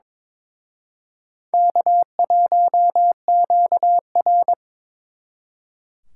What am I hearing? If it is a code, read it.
Morse code, heard as a keyed tone, about 26 wpm, 700 Hz. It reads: K1QR
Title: Speed Racing - USA 1x2 Call Signs 22wpm